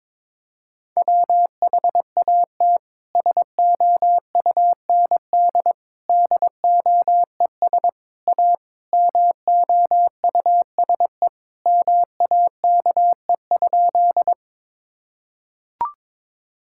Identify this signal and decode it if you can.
Morse code, heard as a keyed tone, about 22 wpm, 700 Hz. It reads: WHAT SOUND DOES A MOUSE MAKE?
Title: Sentences from Top 1000 Words - Review Part 5 - 22wpm